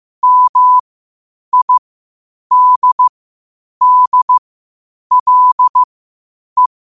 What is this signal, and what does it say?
Morse code, heard as a keyed tone, about 15 wpm, spaced as for 8 wpm, 1000 Hz. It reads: MIDDLE
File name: 8wpm